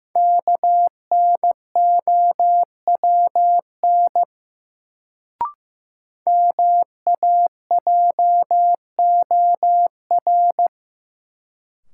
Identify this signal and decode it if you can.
Morse code, heard as a keyed tone, about 15 wpm, 700 Hz. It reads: KNOWN MAJOR
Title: Sentences from Top 600 Words - New Word Drill - 15wpm